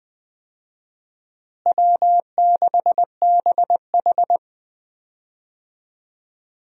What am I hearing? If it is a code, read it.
Morse code, heard as a keyed tone, about 20 wpm, 700 Hz. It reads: W6BH
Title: USA Call Signs - 4 Characters - Repeated 2x - 20wpm